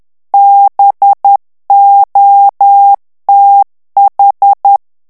Morse code, heard as a keyed tone, about 10 wpm, 800 Hz. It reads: BOTH
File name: ANGLAIS texte clair 3 vitesse 10 mots minute